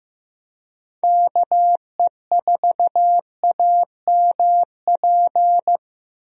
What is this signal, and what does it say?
Morse code, heard as a keyed tone, about 15 wpm, 700 Hz. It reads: KE4AMP
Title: USA Call Signs - 6 Characters - Repeated 2x - 15wpm